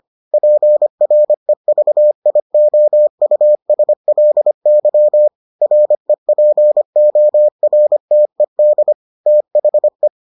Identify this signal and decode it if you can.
Morse code, heard as a keyed tone, about 25 wpm, 600 Hz. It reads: PREVIOUSLY REPORTED THE